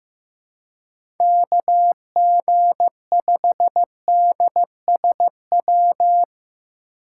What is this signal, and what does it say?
Morse code, heard as a keyed tone, about 15 wpm, 700 Hz. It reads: KG5DSW
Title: USA Call Signs - 4x Character Spacing - 15wpm